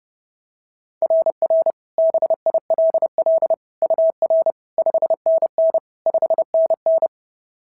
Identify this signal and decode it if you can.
Morse code, heard as a keyed tone, about 30 wpm, 690 Hz. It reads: RR BILL UR 5NN 5NN
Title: SOTA QSO Practice 30wpm